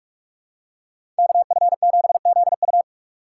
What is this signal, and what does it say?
Morse code, heard as a keyed tone, about 42 wpm, 700 Hz. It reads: KF7ZU
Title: USA Call Signs - 5 Characters - 6x Character Spacing - 45wpm